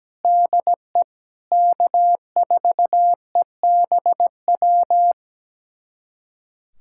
Morse code, heard as a keyed tone, about 17 wpm, 700 Hz. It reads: DE K4EBW
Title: USA Callsign DE Callsign - 5 Characters - 17wpm